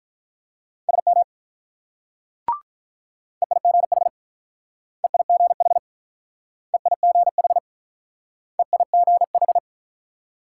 Morse code, heard as a keyed tone, about 47 wpm, 700 Hz. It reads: SM EIGH EIGH EIGH EIGH